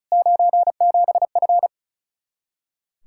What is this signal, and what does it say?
Morse code, heard as a keyed tone, about 43 wpm, 700 Hz. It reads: 97F
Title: Speed-Racing Common QSO Phrases - 35wpm